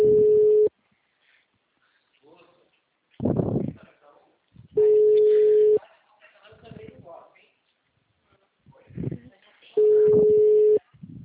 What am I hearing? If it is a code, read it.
Ringback tone (Europe, 425 Hz), with speech.
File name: out- -2007- - - .42381